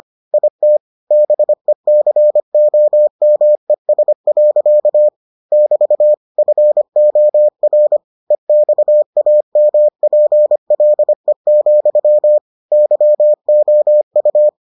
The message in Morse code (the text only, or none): IT BECOMES. <BT> FOR EXAMPLE, YOU